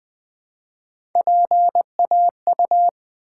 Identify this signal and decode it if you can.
Morse code, heard as a keyed tone, about 20 wpm, 700 Hz. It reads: PAU
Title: Build A Word - 1 Letter At A Time - 3 to 5-Letter Words - 20wpm